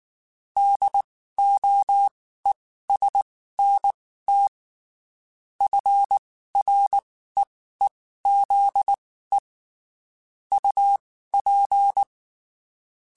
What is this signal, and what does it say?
Morse code, heard as a keyed tone, about 19 wpm, spaced as for 14 wpm, 800 Hz. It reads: DOESNT FREEZE UP
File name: (spaced) / QSOs Spaced-mp3s / WD0FBT @18 sp